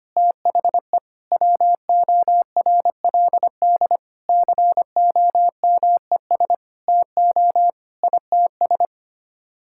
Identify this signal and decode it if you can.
Morse code, heard as a keyed tone, about 25 wpm, 700 Hz. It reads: THE WORLD COMES TO ITS
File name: Sets of 9 Words - 25wpm